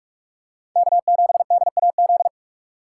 Morse code, heard as a keyed tone, about 46 wpm, 700 Hz. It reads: K7DAZ
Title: USA Call Signs - 5 Characters - 2x Character Spacing - 45wpm